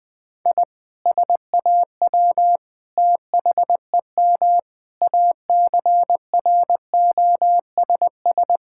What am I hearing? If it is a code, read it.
Morse code, heard as a keyed tone, about 20 wpm, 700 Hz. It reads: I SAW THEM ACROSS
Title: Sentences from Top 1000 Words - Review Part 5 - 20wpm